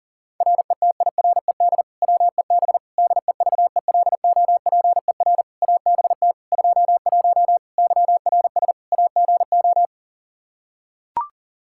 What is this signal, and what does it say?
Morse code, heard as a keyed tone, about 40 wpm, 700 Hz. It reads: RETIRED WEB DEVELOPER ABT 21 YRS AGO
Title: Practice QSO - Exchange 3 - More Info - 40wpm